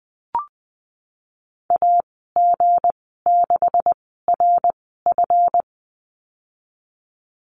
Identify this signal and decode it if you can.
Morse code, heard as a keyed tone, about 20 wpm, spaced as for 15 wpm, 700 Hz. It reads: AG6RF
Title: 179 Lesson 29g - Slash - Call Signs 20wpm